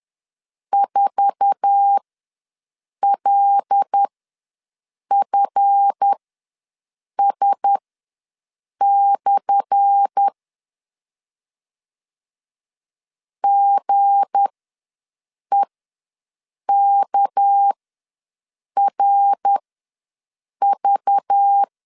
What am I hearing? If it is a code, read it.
Morse code, heard as a keyed tone, about 10 wpm, spaced as for 6 wpm, 800 Hz. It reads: 4LFS/ GEKRV
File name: LEZIONI / 10 SIGNS / 10C DOT ws 10wpm 4213